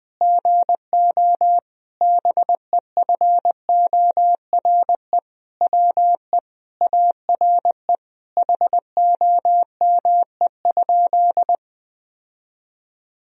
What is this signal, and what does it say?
Morse code, heard as a keyed tone, about 20 wpm, 700 Hz. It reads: GO BEFORE WE ARE HOME?
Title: Sentences from Top 1000 Words - Review Part 1 - 20wpm